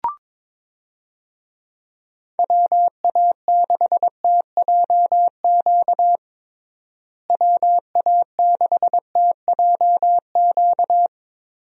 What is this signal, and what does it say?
Morse code, heard as a keyed tone, about 22 wpm, 700 Hz. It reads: WA6TJQ WA6TJQ
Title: USA Call Signs - 6 Characters - Repeated 3x - 22wpm